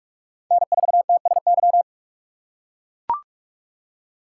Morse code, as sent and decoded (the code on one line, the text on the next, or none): -. ....- - ... -.--
N4TSY